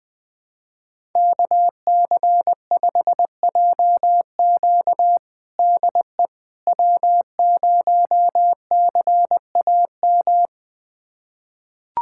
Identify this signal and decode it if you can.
Morse code, heard as a keyed tone, about 20 wpm, 700 Hz. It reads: KC5JQ DE W0CAM
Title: USA Callsign DE Callsign - 5 Characters - 20wpm